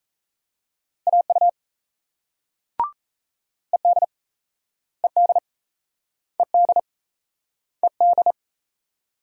Morse code, heard as a keyed tone, about 38 wpm, 700 Hz. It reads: AU ED ED ED ED